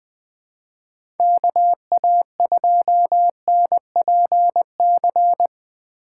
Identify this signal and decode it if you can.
Morse code, heard as a keyed tone, about 20 wpm, 700 Hz. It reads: KA2NPC
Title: USA Call Signs - 6 Characters - 3x Character Spacing - 20wpm